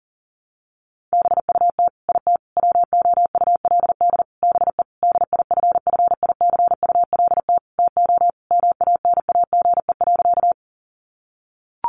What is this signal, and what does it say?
Morse code, heard as a keyed tone, about 40 wpm, 700 Hz. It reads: BUT IT WOULD BE DIFFICULT TO MANAGE.
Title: Sentences from Top 1400 Words 40wpm